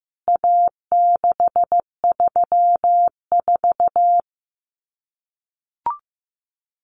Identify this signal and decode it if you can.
Morse code, heard as a keyed tone, about 15 wpm, 700 Hz. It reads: A634
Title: CWT Exchange 15wpm